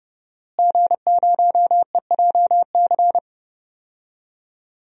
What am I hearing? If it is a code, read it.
Morse code, heard as a keyed tone, about 32 wpm, 700 Hz. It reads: G0EJC
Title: DXCC-100 Call Signs - 5 Characters - Repeated 3x - 30wpm